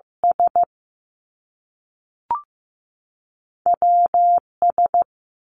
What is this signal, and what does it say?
Morse code, heard as a keyed tone, about 15 wpm, 700 Hz. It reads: S W S
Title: Two Challenging Letter-Number Combinations 15wpm